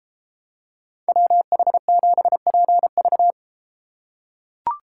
Morse code, heard as a keyed tone, about 32 wpm, 700 Hz. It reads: WH7PV